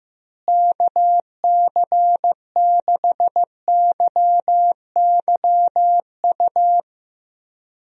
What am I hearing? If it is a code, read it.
Morse code, heard as a keyed tone, about 15 wpm, 700 Hz. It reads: KC6YYU
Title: USA Call Signs - 6 Characters - Repeated 2x - 15wpm